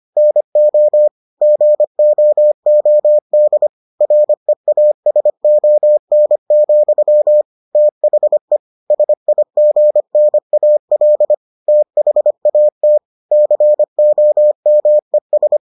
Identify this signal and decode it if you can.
Morse code, heard as a keyed tone, about 25 wpm, 600 Hz. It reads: NO GOOD REASON, THE SIGNAL THAT COMES